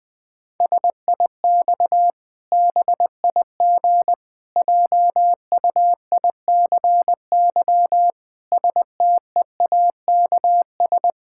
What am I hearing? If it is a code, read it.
Morse code, heard as a keyed tone, about 19 wpm, 700 Hz. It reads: SIX BIG JUICY STEAKS
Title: Pangram Challenge v2 - 20wpm